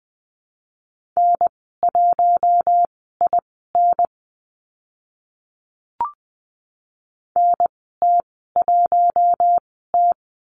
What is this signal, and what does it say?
Morse code, heard as a keyed tone, about 20 wpm, spaced as for 14 wpm, 700 Hz. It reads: N1IN NT1T
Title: 042 Lesson 8g - Number 1 - Call Signs 20wpm